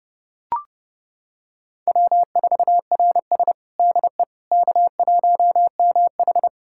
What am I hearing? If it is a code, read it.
Morse code, heard as a keyed tone, about 30 wpm, 700 Hz. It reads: W4RS DE K1MH
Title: USA Callsign DE Callsign - 4 Characters - 30wpm